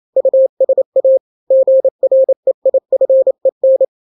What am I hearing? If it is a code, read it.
Morse code, heard as a keyed tone, about 28 wpm, 530 Hz. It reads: USA GREIFEN
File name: daily / 27wpm